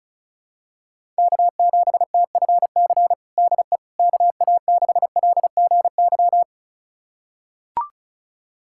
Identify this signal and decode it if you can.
Morse code, heard as a keyed tone, about 34 wpm, 700 Hz. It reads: K7TFC DE KA6LGY
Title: USA Callsign DE Callsign 35wpm